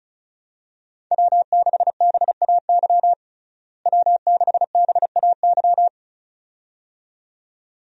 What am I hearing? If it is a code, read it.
Morse code, heard as a keyed tone, about 36 wpm, 700 Hz. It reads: W6BAY W6BAY
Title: USA Call Signs - 5 Characters - Repeated 2x - 35wpm